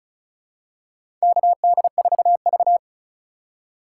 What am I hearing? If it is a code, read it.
Morse code, heard as a keyed tone, about 34 wpm, 700 Hz. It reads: KD4V
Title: USA Call Signs - 4 Characters - 4x Character Spacing - 35wpm